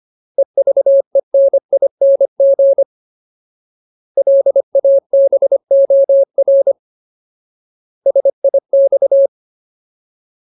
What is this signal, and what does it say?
Morse code, heard as a keyed tone, about 25 wpm, 550 Hz. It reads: EVENING LABOR SIX
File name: Ultimate Word List / 25WPM Ultimate